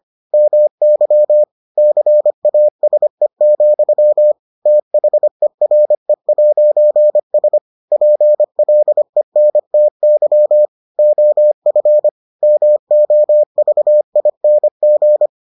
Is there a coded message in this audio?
Yes: Morse code from 0.3 to 15.3 s, heard as a keyed tone, about 25 wpm, 600 Hz. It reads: MY CASE, THERE'S PLENTY OF MOVING